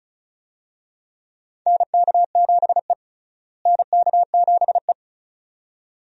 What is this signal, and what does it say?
Morse code, heard as a keyed tone, about 36 wpm, 700 Hz. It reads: NK7E NK7E
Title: USA Call Signs - 4 Characters - Repeated 2x - 35wpm